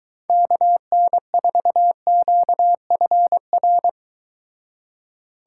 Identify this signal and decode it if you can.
Morse code, heard as a keyed tone, about 23 wpm, 700 Hz. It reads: KN4QFR